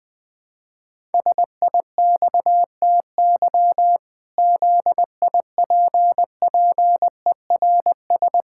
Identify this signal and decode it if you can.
Morse code, heard as a keyed tone, about 20 wpm, 700 Hz. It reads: SIXTY ZIPPERS
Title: Pangram Challenge v2 - 20wpm